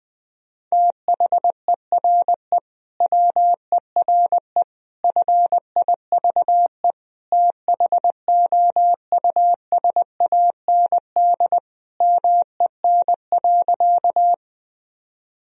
Morse code, heard as a keyed tone, about 20 wpm, 700 Hz. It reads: THERE WERE FIVE THOUSAND MEN.
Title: Sentences from Top 1000 Words - Review Part 3 - 20wpm